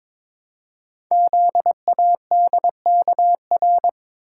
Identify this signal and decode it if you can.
Morse code, heard as a keyed tone, about 22 wpm, 700 Hz. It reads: ZADKR
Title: Code Groups - 5 Letters - 22wpm